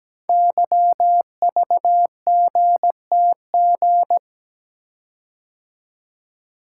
Morse code, heard as a keyed tone, about 17 wpm, 700 Hz. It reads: YVGTG
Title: Code Groups - 5 Letters - 17wpm